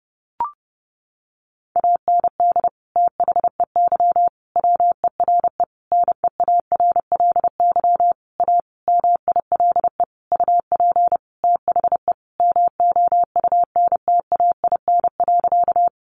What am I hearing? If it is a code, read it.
Morse code, heard as a keyed tone, about 30 wpm, 700 Hz. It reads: AND THEY WERE NEARLY A MILE UP THE MOUNTAIN.
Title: Sentences from Top 1300 Words 30wpm